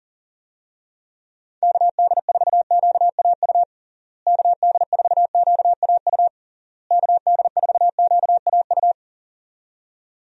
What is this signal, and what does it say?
Morse code, heard as a keyed tone, about 40 wpm, 700 Hz. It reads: KD4QAU KD4QAU KD4QAU
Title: USA Call Signs - 6 Characters - Repeated 3x - 40wpm